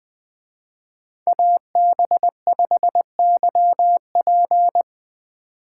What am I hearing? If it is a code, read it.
Morse code, heard as a keyed tone, about 20 wpm, 700 Hz. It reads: AB5YP